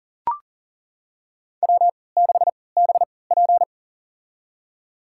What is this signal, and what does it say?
Morse code, heard as a keyed tone, about 40 wpm, spaced as for 23 wpm, 700 Hz. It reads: W6BP
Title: USA Call Signs - 4 Characters - 3x Character Spacing - 40wpm